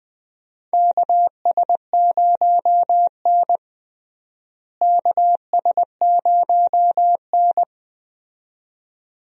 Morse code, heard as a keyed tone, about 20 wpm, 700 Hz. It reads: KS0N KS0N
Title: USA Call Signs - 4 Characters - Repeated 2x - 20wpm